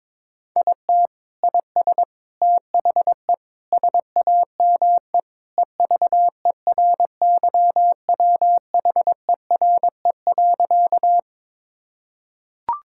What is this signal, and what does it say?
Morse code, heard as a keyed tone, about 22 wpm, 700 Hz. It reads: IT IS THE SAME EVERYWHERE.
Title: Sentences from Top 1600 Words 22wpm